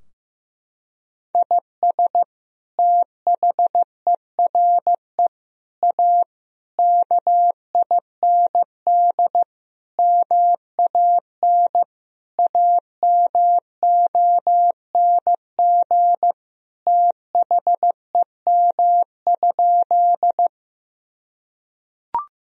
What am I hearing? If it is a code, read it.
Morse code, heard as a keyed tone, about 16 wpm, 700 Hz. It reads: IS THERE A KIND MAN AMONG THEM?
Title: Sentences from Top 1000 Words - Review Part 3 - 15wpm